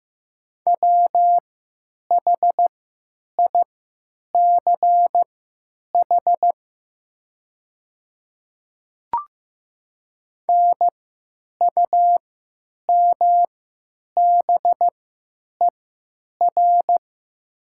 Morse code, heard as a keyed tone, about 15 wpm, spaced as for 9 wpm, 700 Hz. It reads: WHICH NUMBER
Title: 3x Character Spacing - Top 100 Words - 15wpm